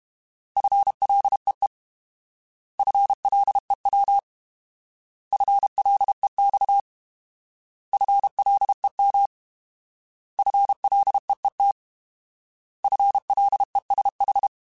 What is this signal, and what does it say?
Morse code, heard as a keyed tone, about 32 wpm, 800 Hz. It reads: FLEE FLEW FLEX FLEM FLEET FLESH